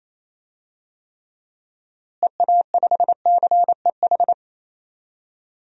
Morse code, heard as a keyed tone, about 27 wpm, 700 Hz. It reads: EA5CEH
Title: DXCC-100 Call Signs - 6 Characters - Repeated 2x - 28wpm